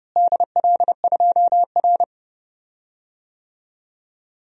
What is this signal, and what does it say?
Morse code, heard as a keyed tone, about 30 wpm, 700 Hz. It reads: DL2R
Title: DXCC-100 Call Signs - 4 Characters - Repeated 3x - 30wpm